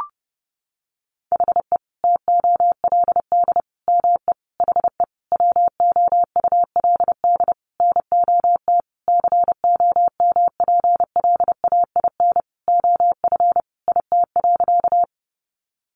Morse code, heard as a keyed tone, about 30 wpm, 700 Hz. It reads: HE TOLD ME HE WOULD NOT COMPLAIN OF IT.
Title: Sentences from Top 1300 Words 30wpm